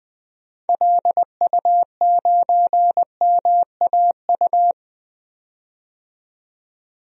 Morse code, heard as a keyed tone, about 20 wpm, 700 Hz. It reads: LU9MAU
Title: DXCC-100 Call Signs - 6 Characters - 20wpm